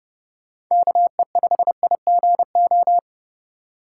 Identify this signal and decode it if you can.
Morse code, heard as a keyed tone, about 29 wpm, 700 Hz. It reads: KE5IGO
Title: USA Call Signs - 6 Characters - 4x Character Spacing - 30wpm